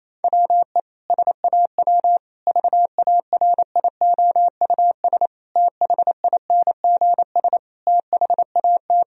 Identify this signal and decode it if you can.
Morse code, heard as a keyed tone, about 28 wpm, 700 Hz. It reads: WE SAW VARIOUS THINGS THAT
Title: Sentences from Top 1000 Words - Review Part 4 - 28wpm